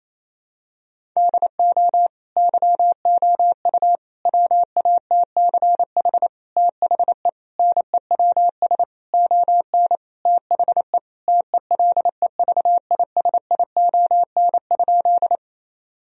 Morse code, heard as a keyed tone, about 28 wpm, 700 Hz. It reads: DO YOU WATCH THE NEWS ON THE TELEVISION?
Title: Sentences from Top 1000 Words - Review Part 4 - 28wpm